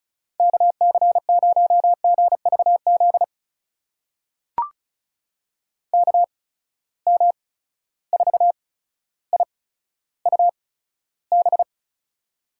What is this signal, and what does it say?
Morse code, heard as a keyed tone, about 36 wpm, spaced as for 15 wpm, 700 Hz. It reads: KC0GVZ KM4IUB
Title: USA Call Signs - 6 Characters - 8x Character Spacing - 35wpm